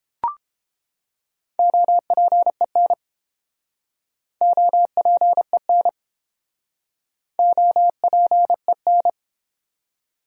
Morse code, heard as a keyed tone, about 32 wpm, 700 Hz. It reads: OPEN OPEN OPEN